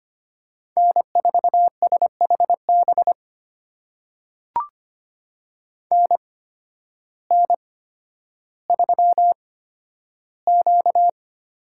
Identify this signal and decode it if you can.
Morse code, heard as a keyed tone, about 25 wpm, spaced as for 12 wpm, 700 Hz. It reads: N4SHB NN3Q
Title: USA Call Signs - 8x Character Spacing - 25wpm